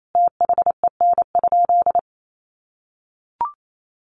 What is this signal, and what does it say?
Morse code, heard as a keyed tone, about 28 wpm, 700 Hz. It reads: THEN?
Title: Sentences from Top 1400 Words 28wpm